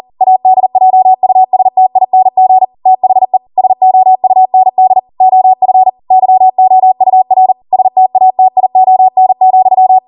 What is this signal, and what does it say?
Morse code, heard as a keyed tone, about 40 wpm, 750 Hz. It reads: ADJUSTING THE SOUND OF YOUR STATION,